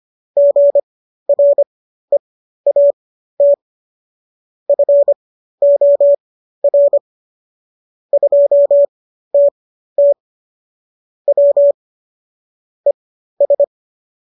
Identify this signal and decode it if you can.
Morse code, heard as a keyed tone, about 25 wpm, spaced as for 13 wpm, 570 Hz. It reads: GREAT FOR 2TT W ES